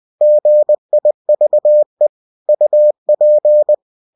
Morse code, heard as a keyed tone, about 20 wpm, 600 Hz. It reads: GIVE UP
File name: cw-books / winnie-the-pooh / winnie-20wpm-page-0037